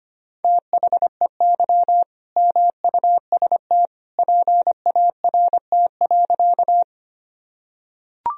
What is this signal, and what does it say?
Morse code, heard as a keyed tone, about 25 wpm, 700 Hz. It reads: THEY MUST PART.
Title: Sentences from Top 1000 Words - Review Part 1 - 25wpm